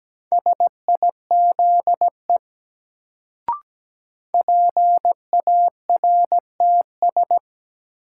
Morse code, heard as a keyed tone, about 17 wpm, 700 Hz. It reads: SIZE PARTS
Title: Sets of 3 Words - New Word Drill - 17wpm